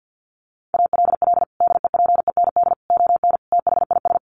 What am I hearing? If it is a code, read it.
Morse code, heard as a keyed tone, about 50 wpm, 700 Hz. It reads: ALL DEPEND ON THIS
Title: Sentences from Top 1300 Words 50wpm